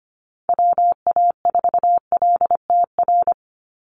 Morse code, heard as a keyed tone, about 25 wpm, 700 Hz. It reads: WA4LTR
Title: USA Call Signs 25wpm